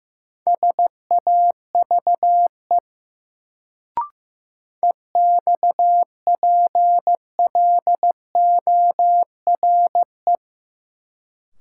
Morse code, heard as a keyed tone, about 15 wpm, 700 Hz. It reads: SAVE EXPLORE
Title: Sentences from Top 1100 Words - New Word Drill - 15wpm